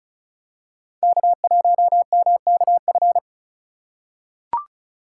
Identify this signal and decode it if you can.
Morse code, heard as a keyed tone, about 34 wpm, 700 Hz. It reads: K1MKF
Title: USA Call Signs - 4x Character Spacing - 35wpm